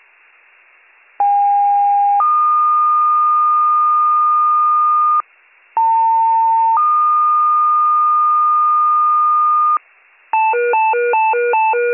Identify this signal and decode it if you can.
Two-tone page, heard as a one-second tone, then a three-second tone, about 810 then 1220 Hz, about 900 then 1220 Hz. It is heard from 1.2 to 9.8 s.